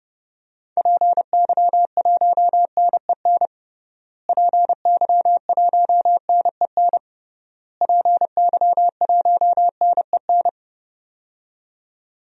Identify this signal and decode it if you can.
Morse code, heard as a keyed tone, about 30 wpm, 700 Hz. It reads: PY1NEN PY1NEN PY1NEN
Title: DXCC-100 Call Signs - 6 Characters - Repeated 3x - 30wpm